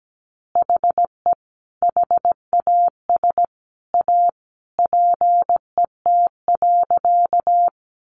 Morse code, heard as a keyed tone, about 17 wpm, 700 Hz. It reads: HE HAS A PET.
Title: Easy Phrases 17wpm